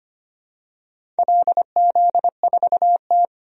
Morse code, heard as a keyed tone, about 25 wpm, 700 Hz. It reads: LZ4T
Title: DXCC-100 Call Signs - 4 Characters - 25wpm